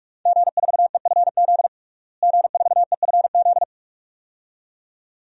Morse code, heard as a keyed tone, about 42 wpm, 700 Hz. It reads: G4EFZ G4EFZ
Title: DXCC-100 Call Signs - 5 Characters - Repeated 2x - 45wpm